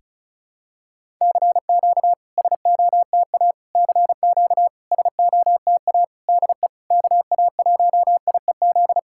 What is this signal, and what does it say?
Morse code, heard as a keyed tone, about 36 wpm, 700 Hz. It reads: CQ SOTA CQ SOTA DE KA1IEZ
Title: SOTA QSO Practice 35wpm